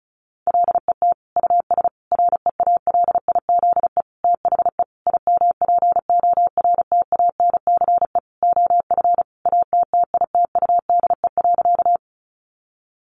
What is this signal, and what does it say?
Morse code, heard as a keyed tone, about 34 wpm, 700 Hz. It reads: LET US REALIZE THE IMPORTANCE OF ATTITUDE.
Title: Sentences from Top 1300 Words 35wpm